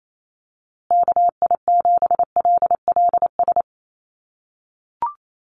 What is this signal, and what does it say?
Morse code, heard as a keyed tone, about 28 wpm, 700 Hz. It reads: KI7LLS
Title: USA Call Signs Repeated 3x 28wpm